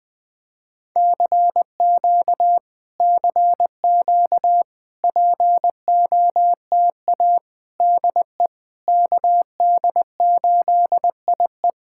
Morse code, heard as a keyed tone, about 21 wpm, 700 Hz. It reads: CQ CQ POTA DE KD8IE
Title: POTA QSO Practice - 20wpm